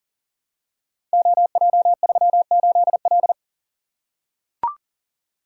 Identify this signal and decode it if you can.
Morse code, heard as a keyed tone, about 43 wpm, 690 Hz. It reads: OJ38L
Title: Code Groups - 5 Letter:Numbers - 40wpm